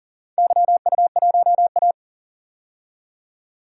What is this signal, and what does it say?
Morse code, heard as a keyed tone, about 40 wpm, 700 Hz. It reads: YU1A
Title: DXCC-100 Call Signs - 4 Characters - 40wpm